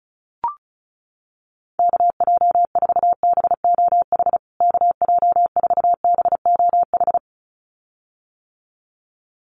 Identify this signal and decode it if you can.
Morse code, heard as a keyed tone, about 34 wpm, 700 Hz. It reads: KJ4BOH KJ4BOH
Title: USA Call Signs Repeated 2x 35wpm